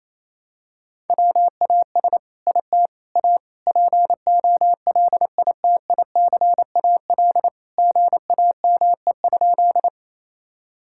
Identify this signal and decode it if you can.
Morse code, heard as a keyed tone, about 28 wpm, 700 Hz. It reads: WAS IT A POLITICAL GAME?
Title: Sentences from Top 1000 Words - Review Part 5 - 28wpm